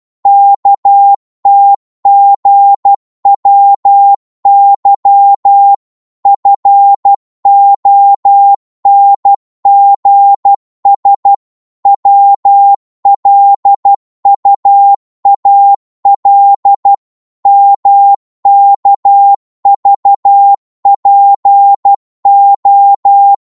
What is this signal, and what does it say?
Morse code, heard as a keyed tone, about 12 wpm, 800 Hz. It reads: KTGWY FONGS WLUAL MKVPO
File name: Morse / ltr-12wpm